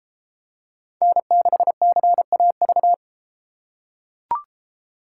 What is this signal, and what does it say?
Morse code, heard as a keyed tone, about 32 wpm, 700 Hz. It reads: N6CAV